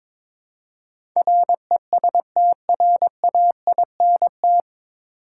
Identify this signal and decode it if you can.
Morse code, heard as a keyed tone, about 22 wpm, 700 Hz. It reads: RESTRAINT
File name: 9 Letter Words - First 3 Letters Given - 22wpm